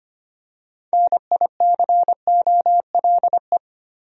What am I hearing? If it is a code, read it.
Morse code, heard as a keyed tone, about 25 wpm, 700 Hz. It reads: NICOLE
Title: USA Common Names - 25wpm